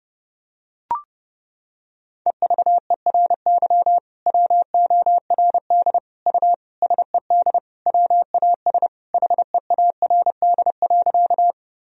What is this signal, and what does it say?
Morse code, heard as a keyed tone, about 32 wpm, 700 Hz. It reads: EVERY WORD U SED WAS HEARD.
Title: Sentences from Top 400 Words and Abbreviations 30wpm